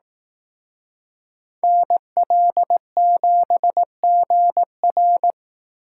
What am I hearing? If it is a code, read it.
Morse code, heard as a keyed tone, about 18 wpm, 700 Hz. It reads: NL7GR
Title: Speed Racing - USA 2x2 Call Signs 15wpm